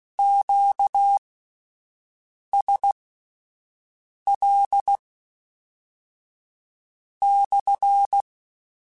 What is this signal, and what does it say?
Morse code, heard as a keyed tone, about 16 wpm, spaced as for 5 wpm, 800 Hz. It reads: QSL/
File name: Q Calls / Q Calls-mp3s / Q Calls SPSL1 @10